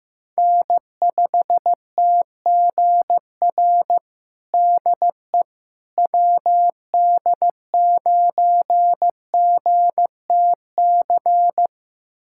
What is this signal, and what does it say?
Morse code, heard as a keyed tone, about 15 wpm, 700 Hz. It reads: N5TGR DE WD9GTC
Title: USA Callsign DE Callsign 15wpm